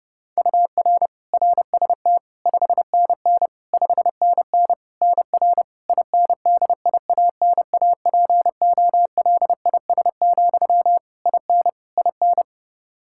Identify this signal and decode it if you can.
Morse code, heard as a keyed tone, about 30 wpm, 700 Hz. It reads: UR RST 5NN 5NN NR INDIANAPOLIS, IN IN
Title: K1USN SST QSO Practice 30wpm